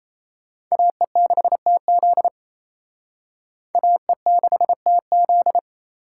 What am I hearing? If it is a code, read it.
Morse code, heard as a keyed tone, about 34 wpm, 700 Hz. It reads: AE6TZ AE6TZ
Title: Speed Racing - USA 2x2 Call Signs 28wpm